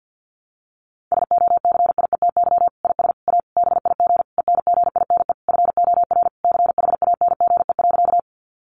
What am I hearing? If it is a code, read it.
Morse code, heard as a keyed tone, about 50 wpm, 700 Hz. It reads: SOCIETY IS A BIG ENGINE FOR CHANGE.
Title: Sentences from Top 1500 Words 50wpm